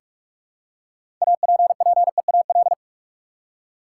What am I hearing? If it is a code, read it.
Morse code, heard as a keyed tone, about 46 wpm, 700 Hz. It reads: APPEAL
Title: Six Letter Words - 45wpm